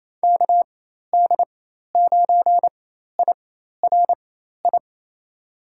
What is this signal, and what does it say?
Morse code, heard as a keyed tone, about 28 wpm, spaced as for 13 wpm, 700 Hz. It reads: KD9IRI
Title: USA Call Signs - 6 Characters - 4x Character Spacing - 28wpm